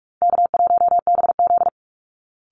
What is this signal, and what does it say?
Morse code, heard as a keyed tone, about 44 wpm, 700 Hz. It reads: K1BZ
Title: Easy Call Signs Repeated 3x - 45wpm